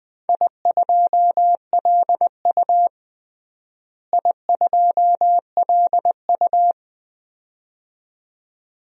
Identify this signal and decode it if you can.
Morse code, heard as a keyed tone, about 20 wpm, 700 Hz. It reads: I2LU I2LU
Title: DXCC-100 Call Signs - 4 Characters - Repeated 2x - 20wpm